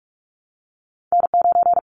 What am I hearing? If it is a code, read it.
Morse code, heard as a keyed tone, about 44 wpm, 700 Hz. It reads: N9
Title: Callsign Prefixes for Top 25 DXCC - 45wpm